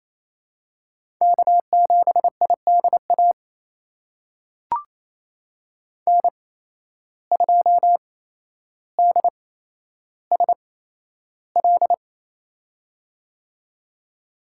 Morse code, heard as a keyed tone, about 28 wpm, spaced as for 12 wpm, 700 Hz. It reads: K7IDA N2DSL
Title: USA Call Signs - 5 Characters - 8x Character Spacing - 28wpm